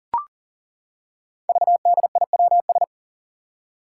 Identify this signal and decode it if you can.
Morse code, heard as a keyed tone, about 40 wpm, 700 Hz. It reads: VDIWS
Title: Five Challenging Letter-Number Combinations 40wpm